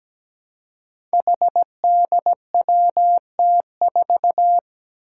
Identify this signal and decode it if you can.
Morse code, heard as a keyed tone, about 17 wpm, 700 Hz. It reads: HDWT4
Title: Five Challenging Letter-Number Combinations 17wpm